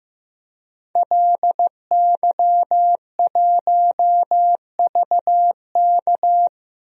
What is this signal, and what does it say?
Morse code, heard as a keyed tone, about 16 wpm, 700 Hz. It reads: LY1VK
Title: DXCC-100 Call Signs - 5 Characters - 15wpm